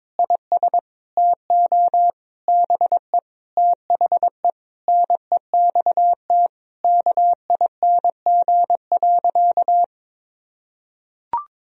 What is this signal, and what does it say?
Morse code, heard as a keyed tone, about 22 wpm, 700 Hz. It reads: IS TO BE THE NEXT KING.
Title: Sentences from Top 1000 Words - Review Part 3 - 22wpm